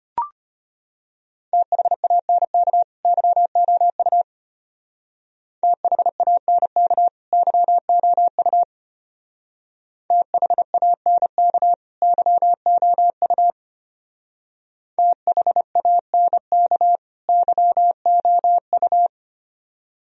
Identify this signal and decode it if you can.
Morse code, heard as a keyed tone, about 32 wpm, 700 Hz. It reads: THANK YOU THANK YOU THANK YOU THANK YOU